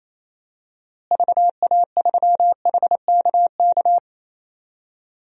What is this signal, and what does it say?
Morse code, heard as a keyed tone, about 28 wpm, 700 Hz. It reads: VA3HKK
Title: DXCC-100 Call Signs - 6 Characters - Repeated 2x - 28wpm